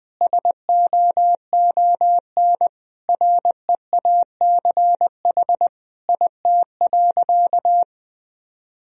Morse code, heard as a keyed tone, about 21 wpm, 700 Hz. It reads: SOON REACH IT.
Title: Sentences from Top 400 Words and Abbreviations 20wpm